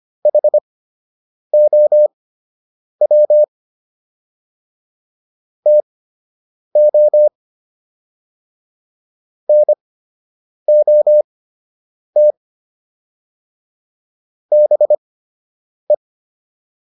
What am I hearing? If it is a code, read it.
Morse code, heard as a keyed tone, about 25 wpm, spaced as for 8 wpm, 600 Hz. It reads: HOW TO NOT BE
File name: WD104 08